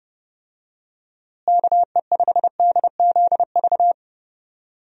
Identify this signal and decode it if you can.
Morse code, heard as a keyed tone, about 30 wpm, 700 Hz. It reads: KE5DZV
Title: USA Call Signs - 6 Characters - Repeated 2x - 30wpm